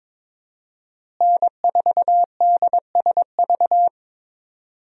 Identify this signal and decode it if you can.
Morse code, heard as a keyed tone, about 22 wpm, 700 Hz. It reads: N4DSV
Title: USA Call Signs - 5 Characters - Repeated 2x - 22wpm